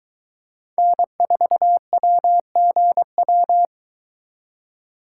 Morse code, heard as a keyed tone, about 22 wpm, 700 Hz. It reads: N4WGW